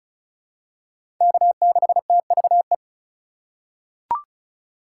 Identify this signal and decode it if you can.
Morse code, heard as a keyed tone, about 34 wpm, 700 Hz. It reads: K6TVE
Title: USA Call Signs - 5 Characters - 6x Character Spacing - 35wpm